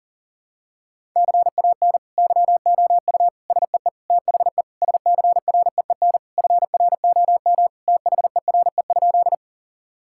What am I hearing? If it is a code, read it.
Morse code, heard as a keyed tone, about 40 wpm, 700 Hz. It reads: CAN YOU SEE THE SCREEN FROM THERE?
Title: Sentences from Top 1000 Words - Review Part 6 - 40wpm